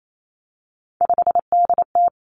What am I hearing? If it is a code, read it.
Morse code, heard as a keyed tone, about 28 wpm, 700 Hz. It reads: 5DT
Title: Three Challenging Letter-Number Combinations 28wpm